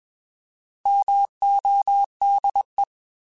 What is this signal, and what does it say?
Morse code, heard as a keyed tone, about 21 wpm, 800 Hz. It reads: MODE
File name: Alphabetical / Alphabetical mp3s / Ms @20